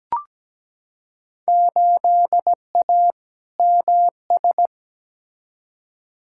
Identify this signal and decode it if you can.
Morse code, heard as a keyed tone, about 17 wpm, 700 Hz. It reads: 8A MS
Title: Field Day Exchanges - 17wpm